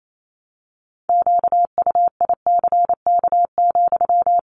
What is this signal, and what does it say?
Morse code, heard as a keyed tone, about 28 wpm, 700 Hz. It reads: QUICK,
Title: Sentences from Top 1300 Words 28wpm